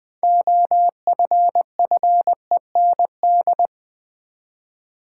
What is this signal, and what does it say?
Morse code, heard as a keyed tone, about 20 wpm, 700 Hz. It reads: OFFEND
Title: 6 Letter Words - First 3 Letters Given - 20wpm